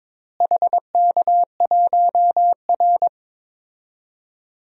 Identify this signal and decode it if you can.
Morse code, heard as a keyed tone, about 22 wpm, 700 Hz. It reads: HK1R
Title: DXCC-100 Call Signs - 4 Characters - Repeated 3x - 22wpm